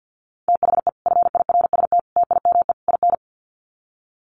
Morse code, heard as a keyed tone, about 50 wpm, 700 Hz. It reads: THE FIRST TIME IN
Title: Sets of 4 Words 50wpm